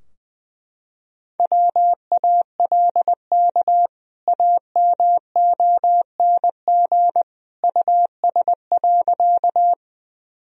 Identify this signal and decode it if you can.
Morse code, heard as a keyed tone, about 20 wpm, 700 Hz. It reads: WALK AMONG US.
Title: Sentences from Top 1000 Words - Review Part 3 - 20wpm